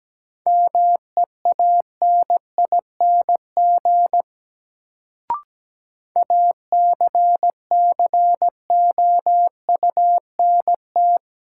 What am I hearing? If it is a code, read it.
Morse code, heard as a keyed tone, about 17 wpm, 700 Hz. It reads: MEANING ACCOUNT
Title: Sentences from Top 900 Words - New Word Drill - 17wpm